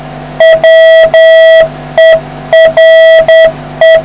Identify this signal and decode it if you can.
Morse code, heard as a keyed tone, about 10 wpm, 650 Hz, with noise. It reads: WERE